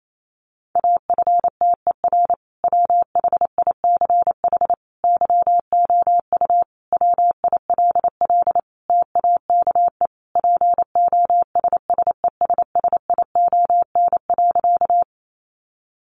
Morse code, heard as a keyed tone, about 28 wpm, 700 Hz. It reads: AFTER WHICH YOU WILL TAKE POSSESSION.
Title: Sentences from Top 1400 Words 28wpm